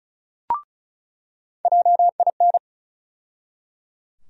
Morse code, heard as a keyed tone, about 34 wpm, 700 Hz. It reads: JIN
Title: USA Call Sign Postfixes - 3 Letters 35wpm